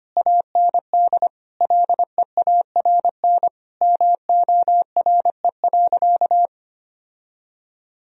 Morse code, heard as a keyed tone, about 25 wpm, 700 Hz. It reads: AND LEARN MORE.
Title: Sentences from Top 1000 Words - Review Part 4 - 25wpm